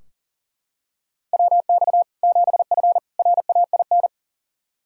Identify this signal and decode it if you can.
Morse code, heard as a keyed tone, about 40 wpm, 700 Hz. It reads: WX 7F RAIN
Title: Practice QSO - Exchange 2 - Info - 40wpm